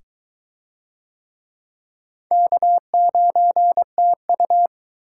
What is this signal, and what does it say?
Morse code, heard as a keyed tone, about 22 wpm, 700 Hz. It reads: K9TU